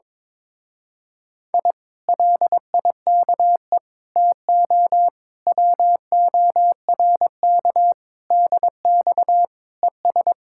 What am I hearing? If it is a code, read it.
Morse code, heard as a keyed tone, about 22 wpm, 700 Hz. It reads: I LIKE TO WORK DX ES
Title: Speed-Racing Common QSO Phrases - 22wpm